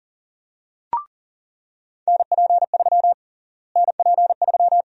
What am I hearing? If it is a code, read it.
Morse code, heard as a keyed tone, about 40 wpm, 700 Hz. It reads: NP3 NP3
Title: USA Call Sign Prefixes - Repeated 2x - 40wpm